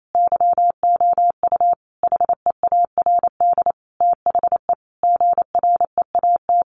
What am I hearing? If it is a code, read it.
Morse code, heard as a keyed tone, about 28 wpm, 700 Hz. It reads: YOU HEARD THE GREAT
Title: Sentences from Top 800 Words 28wpm